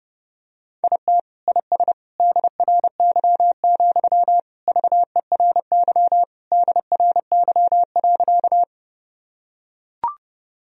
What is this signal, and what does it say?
Morse code, heard as a keyed tone, about 30 wpm, 700 Hz. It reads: IT IS DRY, VERY DRY.
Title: Sentences from Top 1000 Words - Review Part 3 - 30wpm